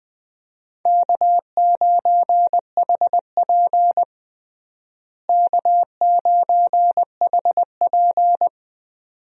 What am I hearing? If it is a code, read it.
Morse code, heard as a keyed tone, about 20 wpm, 700 Hz. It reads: K9HP K9HP
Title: USA Call Signs - 4 Characters - Repeated 2x - 20wpm